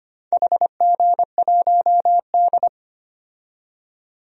Morse code, heard as a keyed tone, about 25 wpm, 700 Hz. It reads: HG1D
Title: DXCC-100 Call Signs - 4 Characters - 25wpm